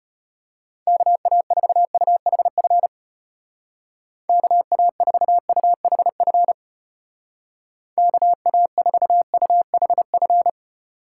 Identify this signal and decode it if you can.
Morse code, heard as a keyed tone, about 34 wpm, 700 Hz. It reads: KA4UHF KA4UHF KA4UHF